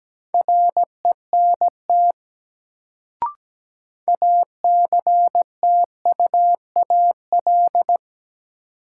Morse code, heard as a keyed tone, about 17 wpm, 700 Hz. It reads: RENT ACTUAL
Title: Sentences from Top 1000 Words - Review Part 5 - New Word Drill - 17wpm